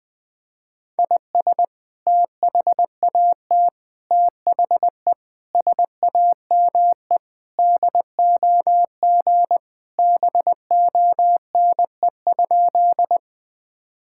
Morse code, heard as a keyed tone, about 20 wpm, 700 Hz. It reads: IS THAT THE SAME DOG BONE?
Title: Sentences from Top 1600 Words 20wpm